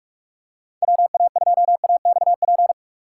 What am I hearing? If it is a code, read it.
Morse code, heard as a keyed tone, about 46 wpm, 700 Hz. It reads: WA2AXP